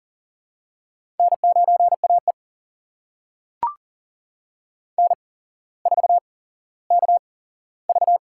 Morse code, heard as a keyed tone, about 38 wpm, spaced as for 17 wpm, 700 Hz. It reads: N9AE N4KV
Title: USA Call Signs - 4 Characters - 8x Character Spacing - 40wpm